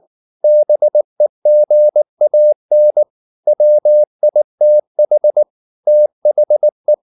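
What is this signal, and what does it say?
Morse code, heard as a keyed tone, about 19 wpm, 600 Hz. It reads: BEGAN WITH THE
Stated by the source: Morse code at 19 wpm from CW med Gnister, reading Ken Follett's novel Never in English.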